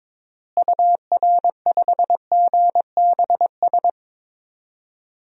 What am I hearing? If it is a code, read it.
Morse code, heard as a keyed tone, about 22 wpm, 700 Hz. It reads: UR5GBS
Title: DXCC-100 Call Signs - 6 Characters - Repeated 2x - 22wpm